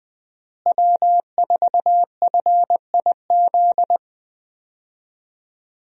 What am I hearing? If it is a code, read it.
Morse code, heard as a keyed tone, about 20 wpm, 700 Hz. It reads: W4FIZ
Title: USA Call Signs - 5 Characters - Repeated 2x - 20wpm